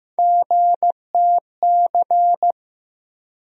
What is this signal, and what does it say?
Morse code, heard as a keyed tone, about 15 wpm, 700 Hz. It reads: GTC
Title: USA Call Sign Postfixes - 3 Letters 15wpm